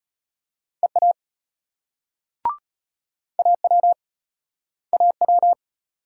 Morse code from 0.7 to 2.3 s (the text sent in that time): EA